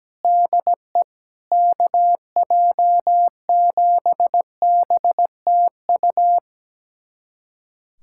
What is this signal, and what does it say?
Morse code, heard as a keyed tone, about 17 wpm, 700 Hz. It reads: DE KJ7BTU
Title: USA Callsign DE Callsign - 6 Characters - 17wpm